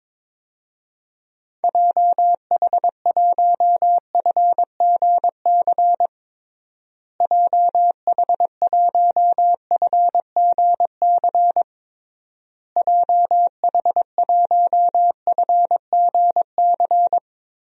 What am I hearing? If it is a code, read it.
Morse code, heard as a keyed tone, about 22 wpm, 700 Hz. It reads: JH1FGC JH1FGC JH1FGC
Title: DXCC-100 Call Signs - 6 Characters - Repeated 3x - 22wpm